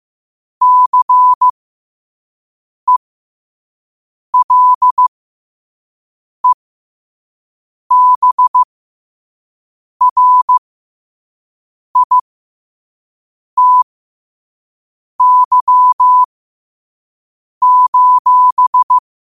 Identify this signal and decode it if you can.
Morse code, heard as a keyed tone, about 15 wpm, spaced as for 5 wpm, 1000 Hz. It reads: CELEBRITY:
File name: qotdfiles / 5wpm